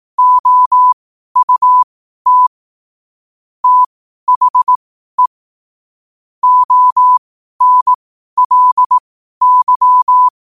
Morse code, heard as a keyed tone, about 18 wpm, spaced as for 12 wpm, 1000 Hz. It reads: OUT THE ONLY
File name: qotdfiles / 12wpm